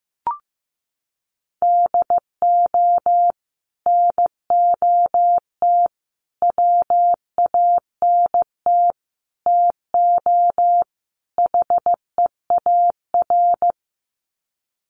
Morse code, heard as a keyed tone, about 15 wpm, 700 Hz. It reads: DO NOT WANT TO HEAR
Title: Sets of 5 Words 15wpm